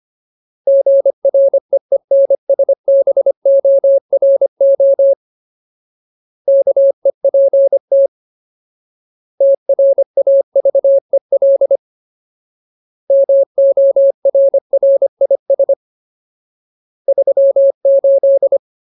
Morse code, heard as a keyed tone, about 25 wpm, 550 Hz. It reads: GREENSBORO KEPT TRAVEL MORRIS 38
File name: Ultimate Word List / 25WPM Ultimate